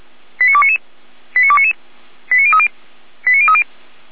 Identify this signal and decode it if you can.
Five-tone selective call (ZVEI), heard as quick runs of five tones, tones of about 70 ms. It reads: 89299 89299 89030 89030